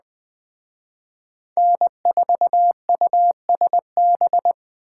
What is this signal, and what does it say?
Morse code, heard as a keyed tone, about 20 wpm, 700 Hz. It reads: N4USB
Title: Speed Racing - USA 1x3 Call Signs 17wpm